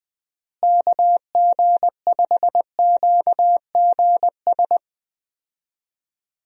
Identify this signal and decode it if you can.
Morse code, heard as a keyed tone, about 20 wpm, 700 Hz. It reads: KG5QGS
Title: USA Call Signs - 6 Characters - Repeated 3x - 20wpm